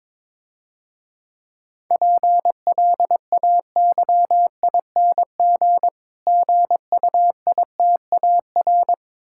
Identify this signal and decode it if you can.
Morse code, heard as a keyed tone, about 23 wpm, 700 Hz. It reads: PLAYING GUITAR
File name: Easy Words - Speed-Racing with Hints - 22wpm